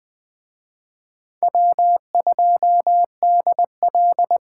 Morse code, heard as a keyed tone, about 21 wpm, 700 Hz. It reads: W2DL
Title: Speed Racing - USA 1x2 Call Signs 17wpm